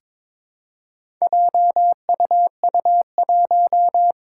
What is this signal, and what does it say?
Morse code, heard as a keyed tone, about 22 wpm, 700 Hz. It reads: JUU1
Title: Four Challenging Letter-Number Combinations 22wpm